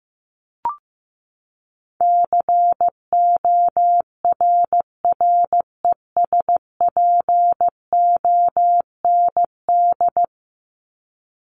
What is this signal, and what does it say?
Morse code, heard as a keyed tone, about 15 wpm, 700 Hz. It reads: CORRESPOND
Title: ING Suffix - Top 100 Words with Base Words 15wpm